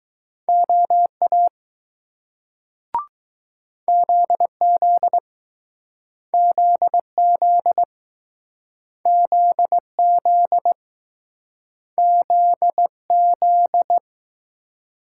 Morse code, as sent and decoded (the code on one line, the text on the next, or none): --- .- / --.. --.. / --.. --.. / --.. --.. / --.. --..
OA ZZ ZZ ZZ ZZ